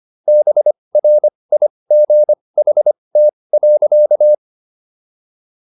Morse code, heard as a keyed tone, about 25 wpm, spaced as for 20 wpm, 600 Hz. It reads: BRIGHT.